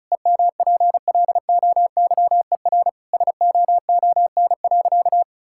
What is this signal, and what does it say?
Morse code, heard as a keyed tone, about 34 wpm, 700 Hz. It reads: EMPLOYER SOON.
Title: Sentences from Top 1600 Words 35wpm